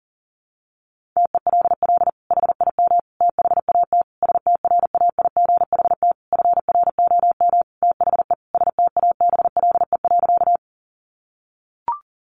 Morse code, heard as a keyed tone, about 40 wpm, 700 Hz. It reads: TELL HIM THAT STRAIGHT FROM THE STABLE.
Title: Sentences from Top 1500 Words 40wpm